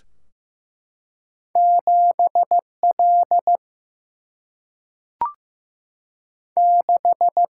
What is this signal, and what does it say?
Morse code, heard as a keyed tone, about 15 wpm, 700 Hz. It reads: 7 L 6
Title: Single-Letter Single-Number 15wpm